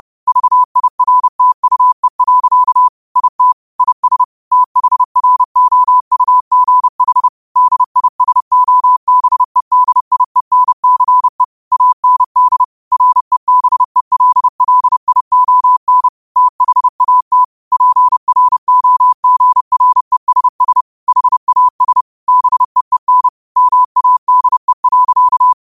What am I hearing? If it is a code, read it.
Morse code, heard as a keyed tone, about 30 wpm, 1000 Hz. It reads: VIRTUE. IT IS THROUGH DISOBEDIENCE AND REBELLION THAT PROGRESS HAS BEEN MADE.